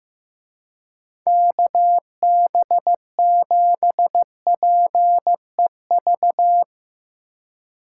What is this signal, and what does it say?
Morse code, heard as a keyed tone, about 16 wpm, 700 Hz. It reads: KB7PEV
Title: Speed Racing - USA 2x3 Call Signs 15wpm